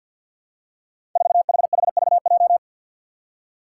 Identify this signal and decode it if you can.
Morse code, heard as a keyed tone, about 46 wpm, 700 Hz. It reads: 4HHVJ
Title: Five Challenging Letter-Number Combinations 50wpm